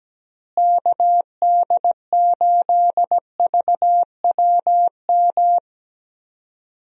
Morse code, heard as a keyed tone, about 17 wpm, 700 Hz. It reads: KD8VWM
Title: USA Call Signs - 6 Characters - Repeated 2x - 17wpm